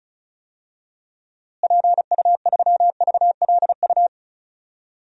Morse code, heard as a keyed tone, about 36 wpm, 700 Hz. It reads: PU3VLU
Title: DXCC-100 Call Signs - 6 Characters - Repeated 2x - 35wpm